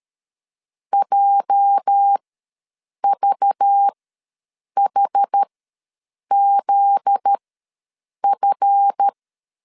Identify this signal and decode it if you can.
Morse code, heard as a keyed tone, about 13 wpm, spaced as for 7 wpm, 800 Hz. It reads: JVHZF
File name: LEZIONI / 8 YJ45 / 8 YJ ws 12wpm 3342